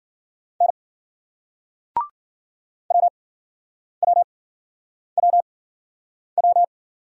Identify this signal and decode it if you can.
Morse code, heard as a keyed tone, about 53 wpm, spaced as for 9 wpm, 700 Hz. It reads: N WWWW